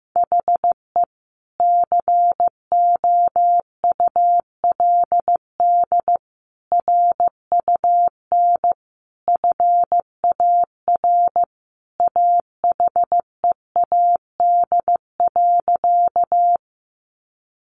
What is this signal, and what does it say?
Morse code, heard as a keyed tone, about 14 wpm, 700 Hz. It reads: HE COULD RUN FAR AHEAD.
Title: Sentences from Top 1100 Words 15wpm